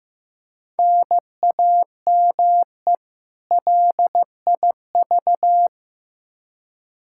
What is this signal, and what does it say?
Morse code, heard as a keyed tone, about 15 wpm, 700 Hz. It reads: NAME LIV
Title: Practice QSO - Exchange 1 - Beginning - 15wpm